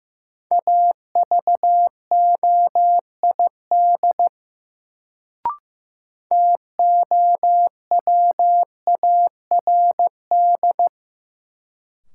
Morse code, heard as a keyed tone, about 15 wpm, 700 Hz. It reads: AVOID TOWARD
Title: Sentences from Top 1000 Words - Review Part 6 - New Word Drill - 15wpm